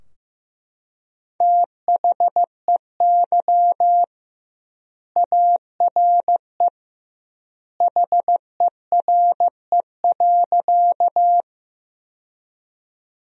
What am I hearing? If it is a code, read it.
Morse code, heard as a keyed tone, about 15 wpm, 700 Hz. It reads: THEY ARE HERE.
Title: 2x Word-Spacing - Sentences from Top 200 Words - 15wpm